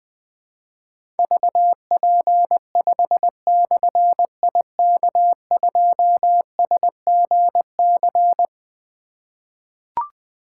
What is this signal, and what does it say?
Morse code, heard as a keyed tone, about 21 wpm, 700 Hz. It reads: VP5/IK2SGC
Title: Difficult Call Signs 20wpm